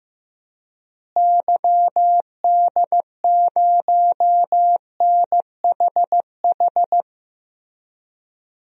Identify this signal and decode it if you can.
Morse code, heard as a keyed tone, about 15 wpm, 700 Hz. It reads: YD0NHH
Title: DXCC-100 Call Signs - 6 Characters - Repeated 2x - 15wpm